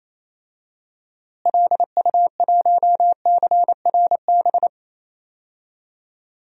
Morse code, heard as a keyed tone, about 28 wpm, 700 Hz. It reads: LU1CRB
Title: DXCC-100 Call Signs - 6 Characters - Repeated 2x - 28wpm